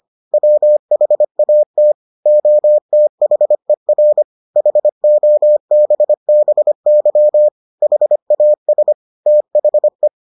Morse code, heard as a keyed tone, about 25 wpm, 600 Hz. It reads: WHAT OTHER HOBBY HAS THE